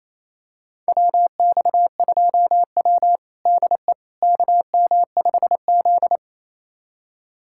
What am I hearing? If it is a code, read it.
Morse code, heard as a keyed tone, about 28 wpm, 700 Hz. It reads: WX2W DE KM5Z
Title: USA Callsign DE Callsign - 4 Characters - 28wpm